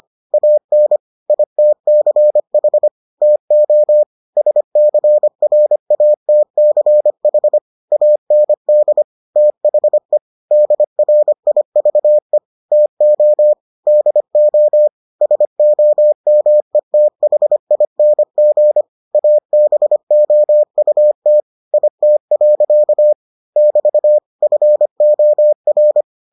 Morse code, heard as a keyed tone, about 25 wpm, 600 Hz. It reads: AN ITCH TO SCRATCH AND THE DRIVE TO DO SOMETHING ABOUT IT. <BT> FOR